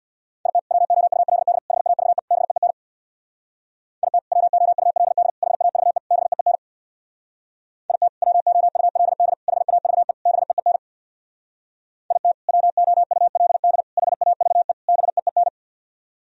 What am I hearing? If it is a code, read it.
Morse code, heard as a keyed tone, about 64 wpm, 700 Hz. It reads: IT WOULD HAVE BEEN IT WOULD HAVE BEEN IT WOULD HAVE BEEN IT WOULD HAVE BEEN